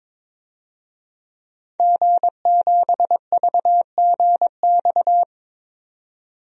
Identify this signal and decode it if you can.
Morse code, heard as a keyed tone, about 22 wpm, 700 Hz. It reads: G7VGX
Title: DXCC-100 Call Signs - 5 Characters - 22wpm